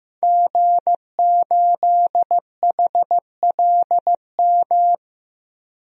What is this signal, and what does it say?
Morse code, heard as a keyed tone, about 15 wpm, 700 Hz. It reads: G8HLM
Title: DXCC-100 Call Signs - 5 Characters - 15wpm